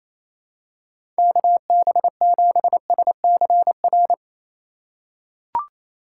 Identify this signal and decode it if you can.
Morse code, heard as a keyed tone, about 28 wpm, 700 Hz. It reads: KB7SCR
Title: USA Call Signs - 3x Character Spacing - 28wpm